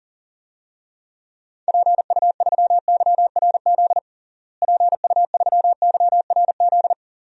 Morse code, heard as a keyed tone, about 40 wpm, 700 Hz. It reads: PU3YRZ PU3YRZ
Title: DXCC-100 Call Signs - 6 Characters - Repeated 3x - 40wpm